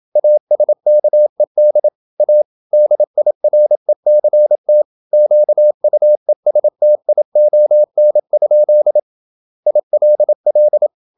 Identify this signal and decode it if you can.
Morse code, heard as a keyed tone, about 27 wpm, 600 Hz. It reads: ASKED A DIRECT QUESTION? ILL
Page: Never 26wpm | CW med Gnister